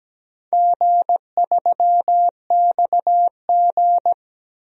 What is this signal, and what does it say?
Morse code, heard as a keyed tone, about 17 wpm, 700 Hz. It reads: G3XG
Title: DXCC-100 Call Signs - 4 Characters - 17wpm